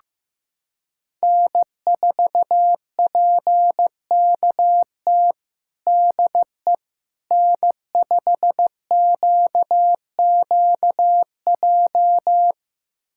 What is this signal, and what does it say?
Morse code, heard as a keyed tone, about 15 wpm, 700 Hz. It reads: N4PKT DE N5QQJ
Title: USA Callsign DE Callsign - 5 Characters - 15wpm